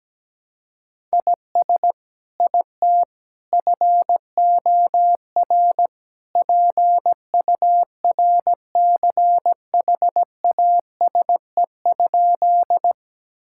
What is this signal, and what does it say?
Morse code, heard as a keyed tone, about 17 wpm, 700 Hz. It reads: IS IT FOR PURCHASE?
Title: Sentences from Top 1000 Words - Review Part 5 - 17wpm